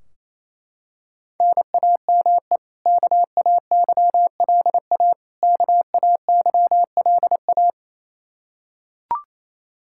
Morse code, heard as a keyed tone, about 28 wpm, 700 Hz. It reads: NAME KAYLA KAYLA
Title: Practice QSO - Exchange 1 - Beginning - 28wpm